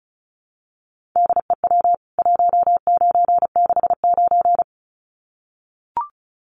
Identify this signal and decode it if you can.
Morse code, heard as a keyed tone, about 34 wpm, 700 Hz. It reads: DEW 1969
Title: CWT Exchange 35wpm